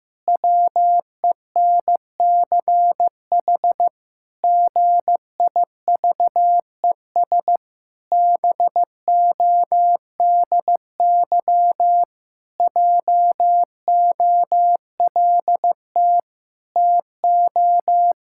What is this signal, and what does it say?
Morse code, heard as a keyed tone, about 15 wpm, 700 Hz. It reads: WENCH GIVES BODY JOLT TO
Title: Pangram Challenge v2 - 15wpm